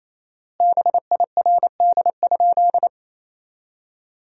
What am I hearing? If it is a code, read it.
Morse code, heard as a keyed tone, about 28 wpm, 700 Hz. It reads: BIRD?
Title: 4x Word-Spacing - Sentences from Top 400 Words - 28wpm